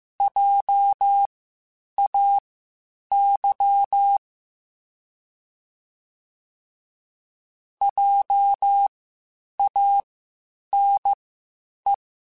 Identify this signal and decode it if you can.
Morse code, heard as a keyed tone, about 15 wpm, spaced as for 8 wpm, 800 Hz. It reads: JAY JANE